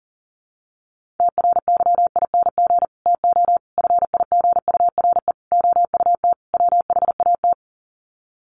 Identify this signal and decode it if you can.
Morse code, heard as a keyed tone, about 40 wpm, 700 Hz. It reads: TRYING TO FIGURE OUT WHAT
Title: Sets of 5 Words 40wpm